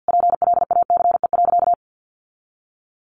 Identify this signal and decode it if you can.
Morse code, heard as a keyed tone, about 49 wpm, 700 Hz. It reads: PLACE.
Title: Sentences from Top 1300 Words 50wpm